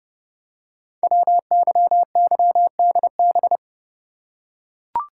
Morse code, heard as a keyed tone, about 30 wpm, 700 Hz. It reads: WYYDB
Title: Code Groups - 5 Letters - 30wpm